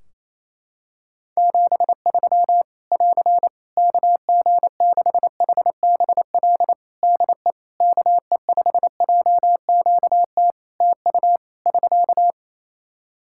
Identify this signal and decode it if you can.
Morse code, heard as a keyed tone, about 27 wpm, 700 Hz. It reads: 73 <AR> KG6HBL DE KE5JQT TU <SK>
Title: Practice QSO - Exchange 4 - Ending - 28wpm